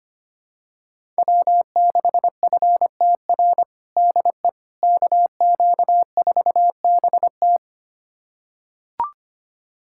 Morse code, heard as a keyed tone, about 25 wpm, 700 Hz. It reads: W6FTR DE KQ4BT
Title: USA Callsign DE Callsign - 5 Characters - 25wpm